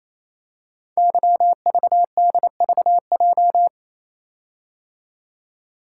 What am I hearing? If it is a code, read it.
Morse code, heard as a keyed tone, about 28 wpm, 700 Hz. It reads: YVDVJ
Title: Code Groups - 5 Letters - 28wpm